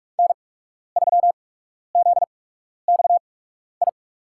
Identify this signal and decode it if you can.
Morse code, heard as a keyed tone, about 46 wpm, spaced as for 12 wpm, 700 Hz. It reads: N3ZXI
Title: USA Call Signs - 5 Characters - 8x Character Spacing - 45wpm